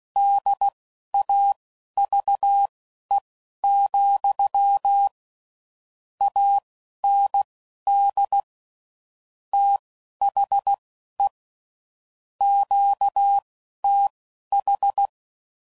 Morse code, heard as a keyed tone, about 16 wpm, spaced as for 11 wpm, 800 Hz. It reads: DAVE, AND THE QTH
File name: CW Code Groups @10WPM / QSOs / QSO kc7sac